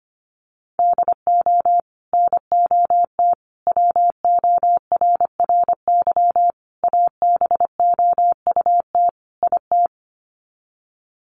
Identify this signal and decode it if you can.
Morse code, heard as a keyed tone, about 25 wpm, 700 Hz. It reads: DO NOT WORRY ABOUT IT
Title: Sets of 5 Words 25wpm